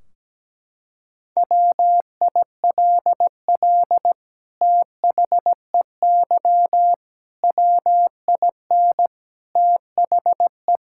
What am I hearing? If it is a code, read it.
Morse code, heard as a keyed tone, about 17 wpm, 700 Hz. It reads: WILL THEY WIN THE
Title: Sentences from Top 1600 Words 17wpm